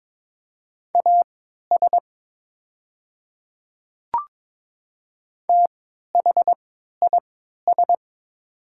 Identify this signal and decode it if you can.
Morse code, heard as a keyed tone, about 22 wpm, spaced as for 12 wpm, 700 Hz. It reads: AS THIS
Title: 3x Character Spacing - Top 100 Words - 22wpm